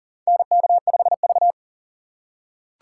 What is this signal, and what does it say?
Morse code, heard as a keyed tone, about 39 wpm, 700 Hz. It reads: NK5V
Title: USA Call Signs - 6x Character Spacing - 40wpm